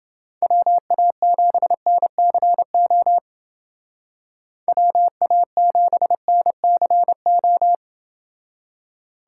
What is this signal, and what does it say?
Morse code, heard as a keyed tone, about 29 wpm, 700 Hz. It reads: WA7NCO WA7NCO